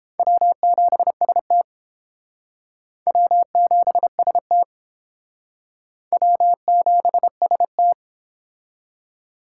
Morse code, heard as a keyed tone, about 28 wpm, 700 Hz. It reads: W7ST W7ST W7ST